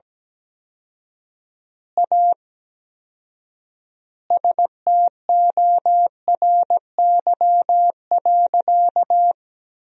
Morse code, heard as a keyed tone, about 17 wpm, 700 Hz. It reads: A STORY.
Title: 4x Word-Spacing - Sentences from Top 300 Words - 17wpm